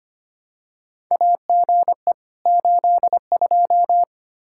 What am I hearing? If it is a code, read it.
Morse code, heard as a keyed tone, about 25 wpm, 700 Hz. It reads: AGE 82
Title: K1USN SST QSO Practice 25wpm